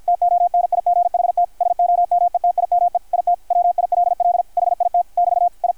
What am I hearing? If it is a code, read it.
Morse code, heard as a keyed tone, about 53 wpm, 660 Hz, with radio noise. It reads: TONIGHT SOMETIME IT WILL HIT <BT> I
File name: archived / CRSnet